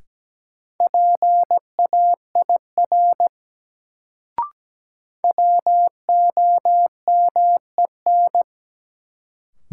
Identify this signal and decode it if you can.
Morse code, heard as a keyed tone, about 17 wpm, 700 Hz. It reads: PAIR WOMEN
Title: Sets of 3 Words - New Word Drill - 17wpm